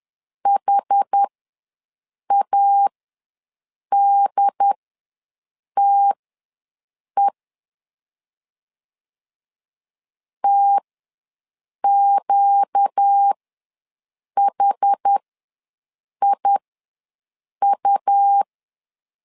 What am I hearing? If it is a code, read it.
Morse code, heard as a keyed tone, about 10 wpm, spaced as for 6 wpm, 800 Hz. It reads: HADTE TQHIU
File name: LEZIONI / 4 Q ws 10wpm 2306